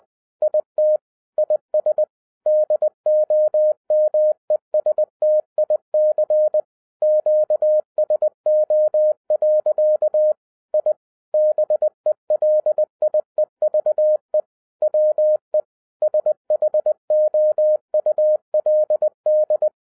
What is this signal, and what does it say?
Morse code, heard as a keyed tone, about 20 wpm, 610 Hz. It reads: IT IS DOMESTIC QSO. I BELIEVE WE SHOULD